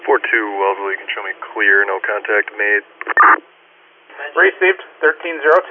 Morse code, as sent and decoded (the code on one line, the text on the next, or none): none